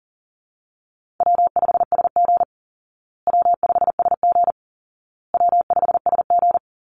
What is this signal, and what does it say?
Morse code, heard as a keyed tone, about 40 wpm, 700 Hz. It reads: W5SG W5SG W5SG
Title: Call Signs Repeated 2x - 40wpm